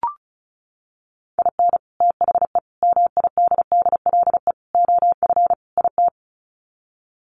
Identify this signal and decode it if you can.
Morse code, heard as a keyed tone, about 34 wpm, 700 Hz. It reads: IN THE MIDDLE OF IT
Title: Sets of 5 Words Encore 35wpm